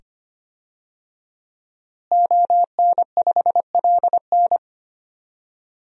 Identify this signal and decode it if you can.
Morse code, heard as a keyed tone, about 25 wpm, 700 Hz. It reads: ON5LN
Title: DXCC-100 Call Signs - 5 Characters - Repeated 2x - 25wpm